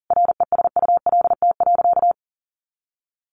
Morse code, heard as a keyed tone, about 37 wpm, 700 Hz. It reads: RESULT.
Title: Sentences from Top 800 Words 40wpm